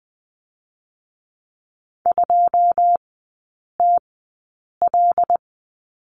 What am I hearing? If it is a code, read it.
Morse code, heard as a keyed tone, about 20 wpm, spaced as for 8 wpm, 700 Hz. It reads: 2TL
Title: 127 Lesson 21e - Number 6 - Three Characters 20wpm